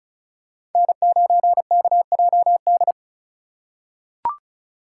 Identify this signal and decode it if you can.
Morse code, heard as a keyed tone, about 34 wpm, 700 Hz. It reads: N9KJD
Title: USA Call Signs - 5 Characters - 3x Character Spacing - 35wpm